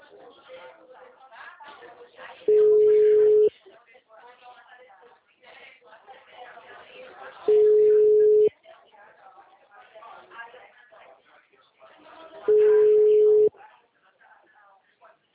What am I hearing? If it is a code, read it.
Ringback tone (Europe, 425 Hz), with speech.